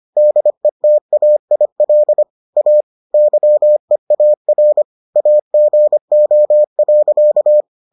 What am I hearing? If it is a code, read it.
Morse code, heard as a keyed tone, about 25 wpm, 600 Hz. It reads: DETAIL A YEAR AGO.